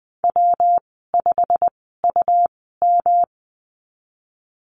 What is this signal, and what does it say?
Morse code, heard as a keyed tone, about 20 wpm, spaced as for 15 wpm, 700 Hz. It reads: W5UM
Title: 128 Lesson 21g - Number 6 - Call Signs 20wpm